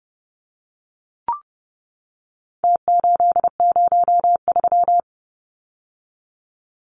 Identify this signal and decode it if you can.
Morse code, heard as a keyed tone, about 30 wpm, 700 Hz. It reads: T803
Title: Numbers - Contest Style - 4-Digit 30wpm